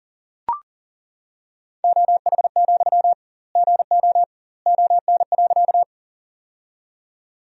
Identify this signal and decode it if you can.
Morse code, heard as a keyed tone, about 40 wpm, 700 Hz. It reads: OH, GO ON.
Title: 2x Word-Spacing - Sentences from Top 500 Words - 40wpm